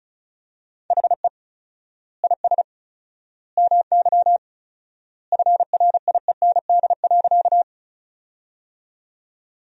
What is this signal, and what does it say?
Morse code, heard as a keyed tone, about 36 wpm, 700 Hz. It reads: HE IS MY FRIEND.
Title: 4x Word-Spacing - Sentences from Top 300 Words - 35wpm